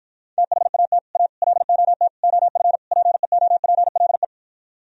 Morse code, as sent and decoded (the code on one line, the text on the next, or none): - .... .- - / .- / .-.. --- - / --- ..-. / .--. --- .--. .-.. .
THAT A LOT OF PEOPLE